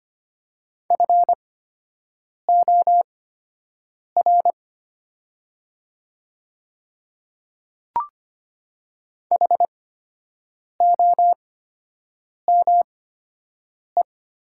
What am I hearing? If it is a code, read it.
Morse code, heard as a keyed tone, about 25 wpm, spaced as for 7 wpm, 700 Hz. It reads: FOR HOME
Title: 8x Character Spacing - Top 300 Words - 25wpm